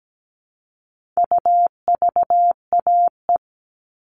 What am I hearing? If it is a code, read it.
Morse code, heard as a keyed tone, about 17 wpm, 700 Hz. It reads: UVAE
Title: All Cut Numbers - 4-Digit 17wpm